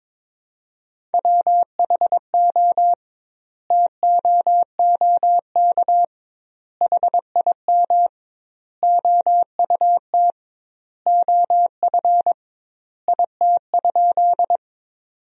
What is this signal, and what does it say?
Morse code, heard as a keyed tone, about 22 wpm, 700 Hz. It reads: WHO TOOK HIM OUT OF IT?
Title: 2x Word-Spacing - Sentences from Top 300 Words - 22wpm